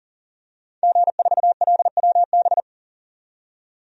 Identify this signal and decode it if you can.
Morse code, heard as a keyed tone, about 40 wpm, 700 Hz. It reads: G4LWB
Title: DXCC-100 Call Signs - 5 Characters - 40wpm